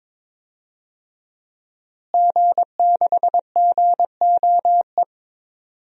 Morse code, heard as a keyed tone, about 22 wpm, 700 Hz. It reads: G6GOE
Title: DXCC-100 Call Signs - 5 Characters - Repeated 2x - 22wpm